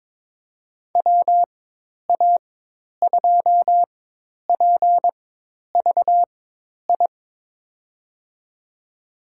Morse code, heard as a keyed tone, about 22 wpm, spaced as for 10 wpm, 700 Hz. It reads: WA2PVI
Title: USA Call Signs - 6 Characters - 4x Character Spacing - 22wpm